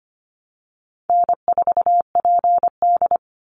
Morse code, heard as a keyed tone, about 25 wpm, 700 Hz. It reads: N4PD
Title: Call Signs Repeated 2x - 25wpm